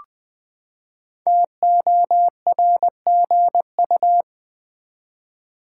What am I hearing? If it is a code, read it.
Morse code, heard as a keyed tone, about 20 wpm, 700 Hz. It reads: TORGU
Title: Code Groups - 5 Letters - 20wpm